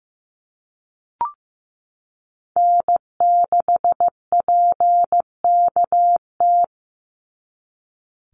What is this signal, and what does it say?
Morse code, heard as a keyed tone, about 15 wpm, 700 Hz. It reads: N6PKT
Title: NA SOTA Chaser Callsigns 15wpm